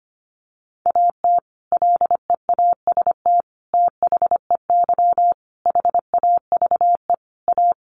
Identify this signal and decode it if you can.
Morse code, heard as a keyed tone, about 25 wpm, 700 Hz. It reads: AT LEAST THEY HAVE A
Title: Sentences from Top 1500 Words 25wpm